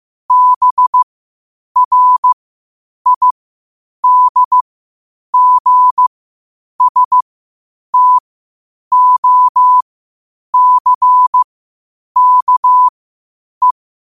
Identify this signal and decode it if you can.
Morse code, heard as a keyed tone, about 15 wpm, spaced as for 8 wpm, 1000 Hz. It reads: BRIDGSTOCKE
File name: qotdfiles / 8wpm